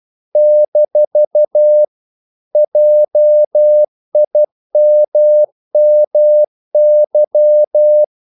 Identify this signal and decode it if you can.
Morse code, heard as a keyed tone, about 12 wpm, 600 Hz. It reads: - JIMMY
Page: English quotations in CW (telegraphy - morse)